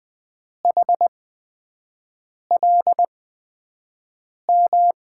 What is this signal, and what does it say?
Morse code, heard as a keyed tone, about 20 wpm, spaced as for 5 wpm, 700 Hz. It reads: HLM
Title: USA Call Signs - 6 Characters - 8x Character Spacing - 20wpm